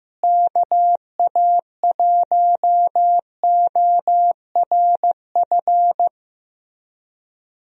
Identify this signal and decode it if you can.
Morse code, heard as a keyed tone, about 16 wpm, 700 Hz. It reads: KA1ORF
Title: USA Call Signs - 6 Characters - Repeated 3x - 15wpm